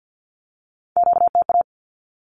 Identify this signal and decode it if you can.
Morse code, heard as a keyed tone, about 50 wpm, 700 Hz. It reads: QTA
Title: Q-Codes 50wpm